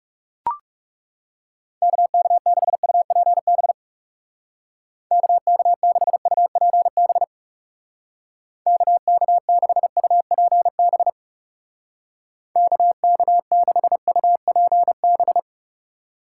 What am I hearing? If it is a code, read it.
Morse code, heard as a keyed tone, about 36 wpm, 700 Hz. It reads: KK6UPB KK6UPB KK6UPB KK6UPB
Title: Speed Racing - USA 2x3 Call Signs 30wpm